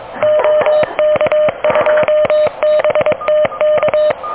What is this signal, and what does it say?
Morse code, heard as a keyed tone, about 24 wpm, 610 Hz. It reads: OK2BTK
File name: OK QRP 2023 / OK QRP 2023